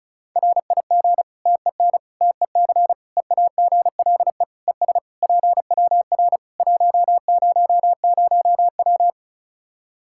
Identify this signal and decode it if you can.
Morse code, heard as a keyed tone, about 36 wpm, 700 Hz. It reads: RIG TEN TEC EAGLE ES PWR 100W
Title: Practice QSO - Exchange 2 - Info - 35wpm